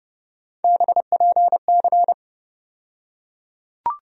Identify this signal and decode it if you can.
Morse code, heard as a keyed tone, about 30 wpm, 700 Hz. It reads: BPC
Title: USA Call Sign Postfixes - 3 Letters 30wpm